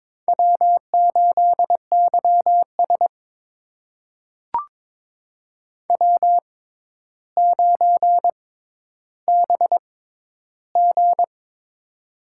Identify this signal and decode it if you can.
Morse code, heard as a keyed tone, about 22 wpm, spaced as for 11 wpm, 700 Hz. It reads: W8YS W9BG
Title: USA Call Signs - 4 Characters - 6x Character Spacing - 22wpm